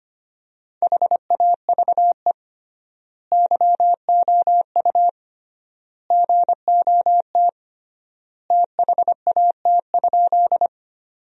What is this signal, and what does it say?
Morse code, heard as a keyed tone, about 25 wpm, 700 Hz. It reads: HAVE YOU GOT THAT?
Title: 3x Word-Spacing - Sentences from Top 300 Words - 25wpm